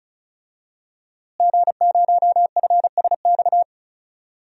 Morse code, heard as a keyed tone, about 36 wpm, 700 Hz. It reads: G0FSX
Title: DXCC-100 Call Signs - 5 Characters - Repeated 3x - 35wpm